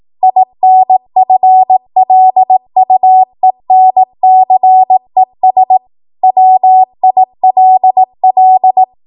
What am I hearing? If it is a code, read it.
Morse code, heard as a keyed tone, about 18 wpm, 750 Hz. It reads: INFLUENCES WILL
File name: Morse / 18WPM